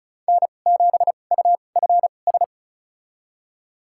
Morse code, heard as a keyed tone, about 34 wpm, spaced as for 25 wpm, 700 Hz. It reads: N7UFS
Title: USA Call Signs - 5 Characters - 2x Character Spacing - 35wpm